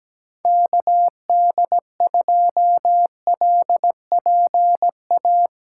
Morse code, heard as a keyed tone, about 17 wpm, 700 Hz. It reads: KD2LPA
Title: USA Call Signs - 6 Characters - 8x Character Spacing - 17wpm